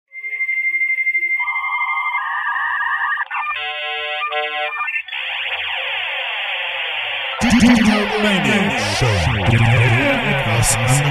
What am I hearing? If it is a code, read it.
Dial-up modem handshake, with speech.